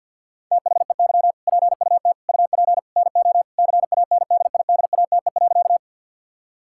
Morse code, heard as a keyed tone, about 50 wpm, 700 Hz. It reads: THEY PUT UP NO CANDIDATE.
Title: Sentences from Top 1600 Words 50wpm